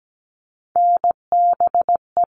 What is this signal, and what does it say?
Morse code, heard as a keyed tone, about 17 wpm, 700 Hz. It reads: NBE
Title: All Cut Numbers - 3-Digit 17wpm